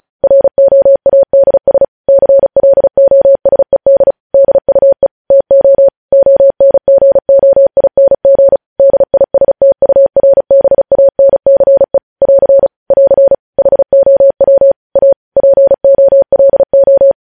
Morse code, heard as a keyed tone, about 36 wpm, 550 Hz. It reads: ROADS CLOSED DUE TO ONGOING DISTURBANCE <AR> <AR> HOW A POLO
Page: news headlines in morse code at 35 wpm